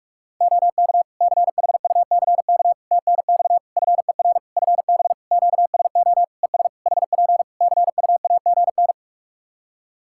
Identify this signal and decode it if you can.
Morse code, heard as a keyed tone, about 48 wpm, 700 Hz. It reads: OK CHUCK TNX FER FB QSO ES HP CUAGN
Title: Practice QSO - Exchange 4 - Ending - 45wpm